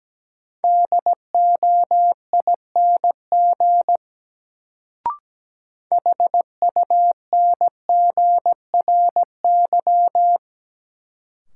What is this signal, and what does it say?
Morse code, heard as a keyed tone, about 17 wpm, 700 Hz. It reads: DOING HUNGRY
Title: Sentences from Top 1400 Words - New Word Drill - 17wpm